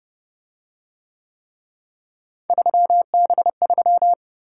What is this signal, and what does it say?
Morse code, heard as a keyed tone, about 30 wpm, 700 Hz. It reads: 3B3